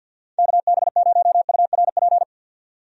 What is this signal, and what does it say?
Morse code, heard as a keyed tone, about 53 wpm, 700 Hz. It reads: KB0URP
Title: USA Call Signs - 6 Characters - 50wpm